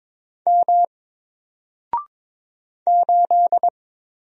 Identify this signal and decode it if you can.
Morse code, heard as a keyed tone, about 22 wpm, 700 Hz. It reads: M 8
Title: Single Character (Letter:Number:Punctuation:Prosign) Alt 22wpm